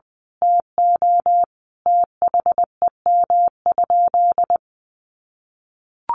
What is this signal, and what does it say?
Morse code, heard as a keyed tone, about 20 wpm, 700 Hz. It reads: TO THEM?
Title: Sentences from Top 1200 Words 20wpm